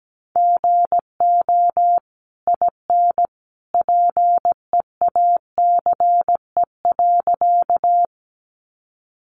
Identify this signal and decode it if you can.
Morse code, heard as a keyed tone, about 17 wpm, 700 Hz. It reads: GO IN PEACE.
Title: Sentences from Top 1500 Words 17wpm